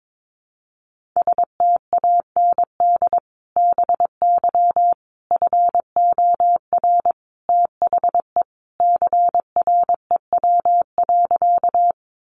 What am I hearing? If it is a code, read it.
Morse code, heard as a keyed tone, about 22 wpm, 700 Hz. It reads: STAND BY FOR THE CREW.
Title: Sentences from Top 1500 Words 22wpm